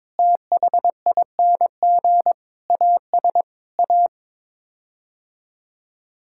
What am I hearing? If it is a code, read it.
Morse code, heard as a keyed tone, about 22 wpm, 700 Hz. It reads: THING AS A
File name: practice / 22wpm / Sets of 8 Words - 22wpm